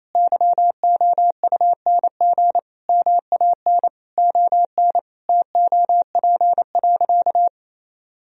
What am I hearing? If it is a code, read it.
Morse code, heard as a keyed tone, about 28 wpm, 700 Hz. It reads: YOUNG MAN ON TOP.
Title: Sentences from Top 1000 Words - Review Part 2 - 28wpm